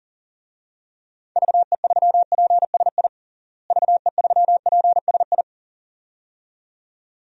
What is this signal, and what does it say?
Morse code, heard as a keyed tone, about 37 wpm, 700 Hz. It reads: VE3PSI VE3PSI
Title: DXCC-100 Call Signs - 6 Characters - Repeated 2x - 40wpm